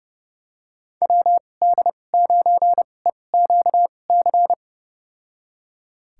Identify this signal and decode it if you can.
Morse code, heard as a keyed tone, about 30 wpm, spaced as for 22 wpm, 700 Hz. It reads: WD9EQC
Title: USA Call Signs - 6 Characters - 2x Character Spacing - 30wpm